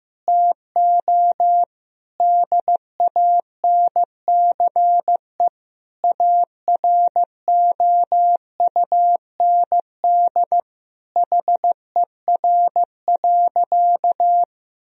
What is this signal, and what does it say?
Morse code, heard as a keyed tone, about 15 wpm, 700 Hz. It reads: TO DANCE AROUND HER.
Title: Sentences from Top 1000 Words - Review Part 4 - 15wpm